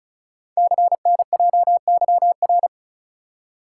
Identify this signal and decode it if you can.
Morse code, heard as a keyed tone, about 36 wpm, 700 Hz. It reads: CNJYR
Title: Code Groups - 5 Letters - 35wpm